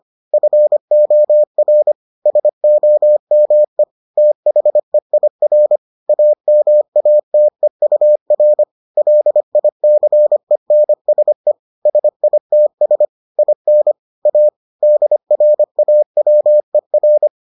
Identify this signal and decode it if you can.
Morse code, heard as a keyed tone, about 25 wpm, 600 Hz. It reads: FOR SOME THEIR AMATEUR LICENSE SITS IN A DRAWER